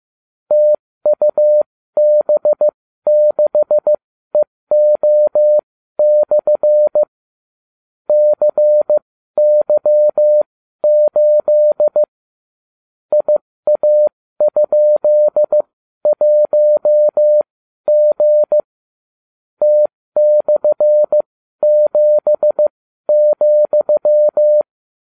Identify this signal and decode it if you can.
Morse code, heard as a keyed tone, about 15 wpm, spaced as for 12 wpm, 600 Hz. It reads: TUB6EO/ CY8 IA?1G T/7,